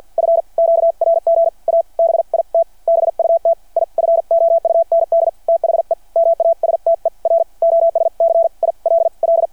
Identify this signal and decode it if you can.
Morse code, heard as a keyed tone, about 53 wpm, spaced as for 43 wpm, 670 Hz. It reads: F QRG A BIT BUT I FOUND THE MASTER OSCILL
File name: archived / CRSnet